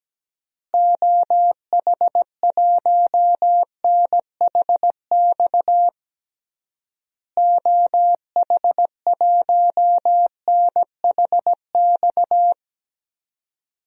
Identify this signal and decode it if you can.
Morse code, heard as a keyed tone, about 17 wpm, 700 Hz. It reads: OH1NHX OH1NHX
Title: DXCC-100 Call Signs - 6 Characters - Repeated 3x - 17wpm